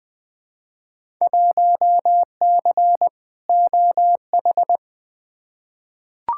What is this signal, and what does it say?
Morse code, heard as a keyed tone, about 20 wpm, 700 Hz. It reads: 1C OH
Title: Field Day Exchanges - 20wpm